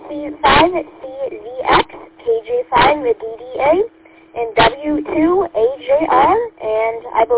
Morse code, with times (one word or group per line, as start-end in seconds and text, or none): none